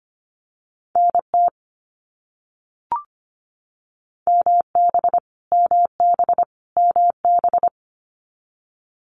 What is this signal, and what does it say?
Morse code, heard as a keyed tone, about 25 wpm, 700 Hz. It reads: NT MB MB MB
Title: Canadian Provinces and Territories - Repeated 3x - 25wpm